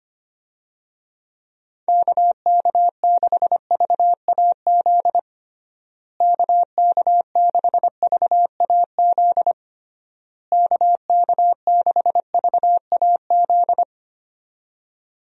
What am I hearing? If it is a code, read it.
Morse code, heard as a keyed tone, about 25 wpm, 700 Hz. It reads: KK6VAZ KK6VAZ KK6VAZ
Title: USA Call Signs - 6 Characters - Repeated 3x - 25wpm